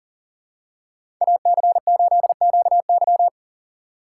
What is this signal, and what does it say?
Morse code, heard as a keyed tone, about 40 wpm, 700 Hz. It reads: AC8QY
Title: USA Call Signs - 3x Character Spacing - 40wpm